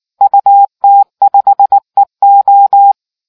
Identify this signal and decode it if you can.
Morse code, heard as a keyed tone, about 19 wpm, 800 Hz. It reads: UT5EO